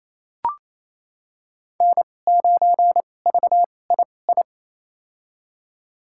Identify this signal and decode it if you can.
Morse code, heard as a keyed tone, about 28 wpm, spaced as for 20 wpm, 700 Hz. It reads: N9VII
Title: USA Call Signs - 5 Characters - 2x Character Spacing - 28wpm